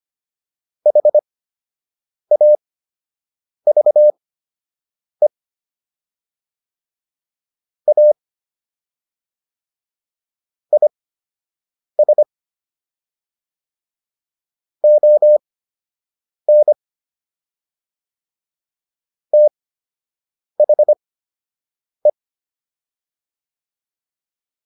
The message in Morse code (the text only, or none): HAVE A IS ON THE